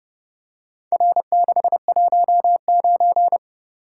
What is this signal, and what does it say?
Morse code, heard as a keyed tone, about 27 wpm, 700 Hz. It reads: R619